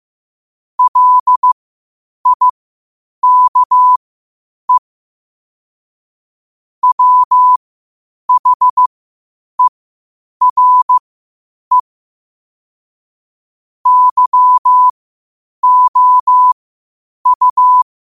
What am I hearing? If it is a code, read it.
Morse code, heard as a keyed tone, about 15 wpm, spaced as for 8 wpm, 1000 Hz. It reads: LIKE WHERE YOU
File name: qotdfiles / 8wpm